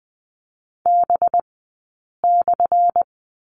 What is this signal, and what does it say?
Morse code, heard as a keyed tone, about 20 wpm, spaced as for 8 wpm, 700 Hz. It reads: B/
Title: 197 Lesson 32d - Letter k - Two Characters 20wpm